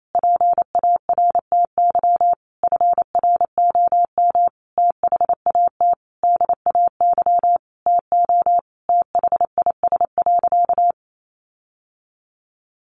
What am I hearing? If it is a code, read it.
Morse code, heard as a keyed tone, about 28 wpm, 700 Hz. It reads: PARTY FROM THAT DAY TO THIS.
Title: Sentences from Top 1400 Words 28wpm